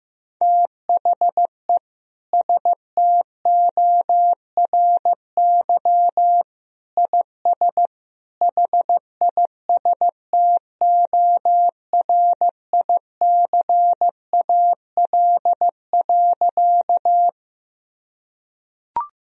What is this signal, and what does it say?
Morse code, heard as a keyed tone, about 15 wpm, 700 Hz. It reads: THE STORY IS HISTORICAL.
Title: Sentences from Top 1000 Words - Review Part 4 - 15wpm